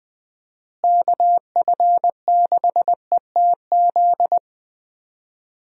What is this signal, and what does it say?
Morse code, heard as a keyed tone, about 20 wpm, 700 Hz. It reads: KF6ETZ
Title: USA Call Signs - 6 Characters - Repeated 3x - 20wpm